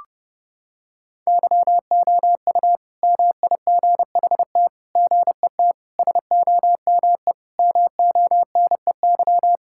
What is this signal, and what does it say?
Morse code, heard as a keyed tone, about 32 wpm, 700 Hz. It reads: YOU MIGHT GET SOME MONEY
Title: Sentences from Top 1000 Words - Review Part 5 - 30wpm